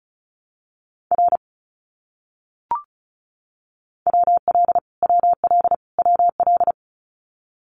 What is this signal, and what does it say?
Morse code, heard as a keyed tone, about 34 wpm, 700 Hz. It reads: R WL WL WL
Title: Abbreviations Repeated 3x - 35wpm